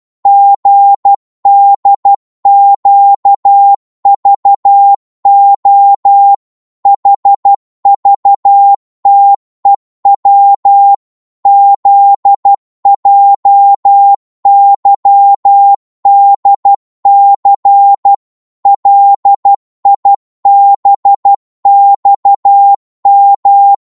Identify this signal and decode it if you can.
Morse code, heard as a keyed tone, about 12 wpm, 800 Hz. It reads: GDQVO HVTEW ZJYDC LIBXM
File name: Morse / ltr-12wpm